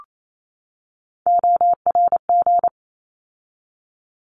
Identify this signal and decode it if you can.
Morse code, heard as a keyed tone, about 28 wpm, 700 Hz. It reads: ORG
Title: ARRL Sections 28wpm